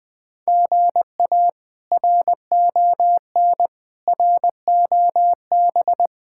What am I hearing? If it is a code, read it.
Morse code, heard as a keyed tone, about 21 wpm, 700 Hz. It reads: GA RON ROB
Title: K1USN SST QSO Practice - 20wpm